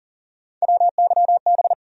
Morse code, heard as a keyed tone, about 40 wpm, 700 Hz. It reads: WYB
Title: USA Call Sign Postfixes - 3 Letters 40wpm